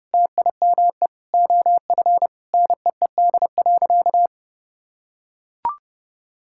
Morse code, heard as a keyed tone, about 30 wpm, 700 Hz. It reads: TIME OF NEED.
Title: Sentences from Top 1000 Words - Review Part 5 - 30wpm